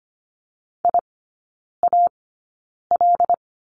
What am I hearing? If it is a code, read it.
Morse code, heard as a keyed tone, about 25 wpm, spaced as for 9 wpm, 690 Hz. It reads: IAL
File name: Lesson 14e - Number 2 - Three Characters 25wpm